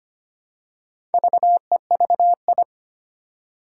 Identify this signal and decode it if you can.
Morse code, heard as a keyed tone, about 25 wpm, 700 Hz. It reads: VEVI
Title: Four Challenging Letter-Number Combinations 25wpm